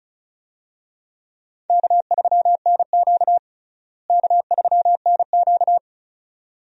Morse code, heard as a keyed tone, about 36 wpm, 700 Hz. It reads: K3NQ K3NQ
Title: USA Call Signs - 4 Characters - Repeated 2x - 35wpm